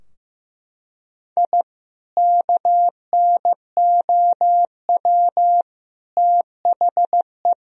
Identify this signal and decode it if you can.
Morse code, heard as a keyed tone, about 16 wpm, 700 Hz. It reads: I KNOW THE
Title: Sentences from Top 1000 Words - Review Part 2 - 15wpm